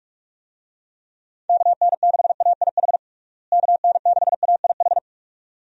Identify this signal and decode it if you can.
Morse code, heard as a keyed tone, about 46 wpm, 700 Hz. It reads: KN6AIH KN6AIH
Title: USA Call Signs - 6 Characters - Repeated 2x - 45wpm